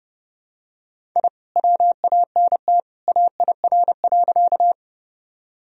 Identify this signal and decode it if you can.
Morse code, heard as a keyed tone, about 30 wpm, 700 Hz. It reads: I WANT AIR.
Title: Sentences from Top 1000 Words - Review Part 1 - 30wpm